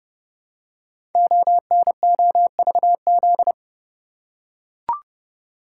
Morse code, heard as a keyed tone, about 28 wpm, 700 Hz. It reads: ONOVZ
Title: Code Groups - 5 Letters - 30wpm